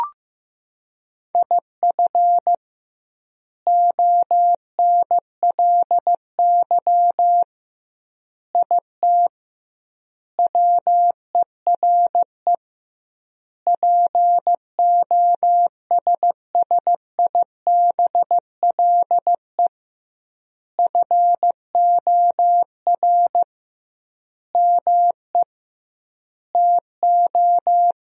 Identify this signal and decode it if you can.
Morse code, heard as a keyed tone, about 15 wpm, 700 Hz. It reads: IF ONLY IT WERE POSSIBLE FOR ME TO
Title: 2x Word-Spacing - Sentences from Top 500 Words - 15wpm